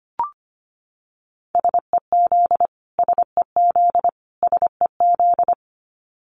Morse code, heard as a keyed tone, about 25 wpm, 700 Hz. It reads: SEZ SEZ SEZ
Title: Abbreviations Repeated 3x - 25wpm